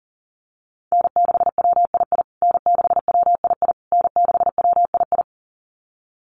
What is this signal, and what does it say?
Morse code, heard as a keyed tone, about 37 wpm, 700 Hz. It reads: N6WII N6WII N6WII
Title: USA Call Signs Repeated 3x 40wpm